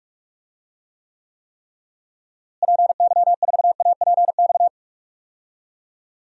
Morse code, heard as a keyed tone, about 45 wpm, 700 Hz. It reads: PY4APX
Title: DXCC-100 Call Signs - 6 Characters - Repeated 3x - 45wpm